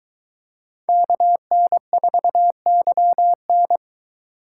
Morse code, heard as a keyed tone, about 23 wpm, 700 Hz. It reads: KN4YN